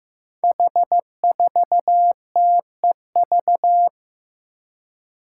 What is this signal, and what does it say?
Morse code, heard as a keyed tone, about 15 wpm, 700 Hz. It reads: H4TEV
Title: Five Challenging Letter-Number Combinations 15wpm